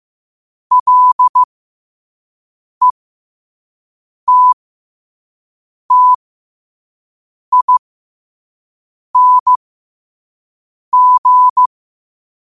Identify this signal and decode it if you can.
Morse code, heard as a keyed tone, about 15 wpm, spaced as for 5 wpm, 1000 Hz. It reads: LETTING
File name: qotdfiles / 5wpm